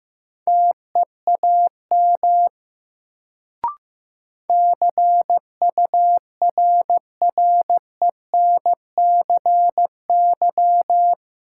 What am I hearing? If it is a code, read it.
Morse code, heard as a keyed tone, about 15 wpm, 700 Hz. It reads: TEAM CURRENCY
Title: Sentences from Top 1500 Words - New Word Drill Alt - 15wpm